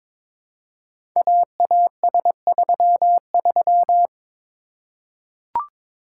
Morse code, heard as a keyed tone, about 23 wpm, 700 Hz. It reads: AAS33
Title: Code Groups - 5 Letter:Numbers - 22wpm